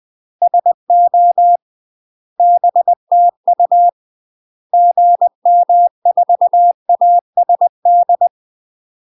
Morse code, heard as a keyed tone, about 20 wpm, 700 Hz. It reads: SO BTU GM4ASD
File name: UK MASTER QSOs TRAINING / 20WPM / UK MASTER QSOs 0029